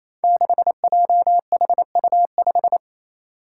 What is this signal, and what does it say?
Morse code, heard as a keyed tone, about 28 wpm, 700 Hz. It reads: 6JHU5
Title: Five Challenging Letter-Number Combinations 28wpm